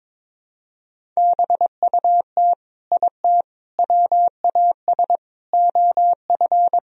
Morse code, heard as a keyed tone, about 22 wpm, 700 Hz. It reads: BUT IT WAS OF
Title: Sentences from Top 100 Words and Abbreviations 22wpm